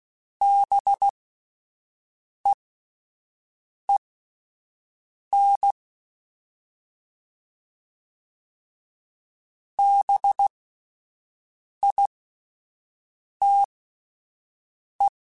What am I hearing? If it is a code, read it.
Morse code, heard as a keyed tone, about 16 wpm, spaced as for 5 wpm, 800 Hz. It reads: BEEN BITE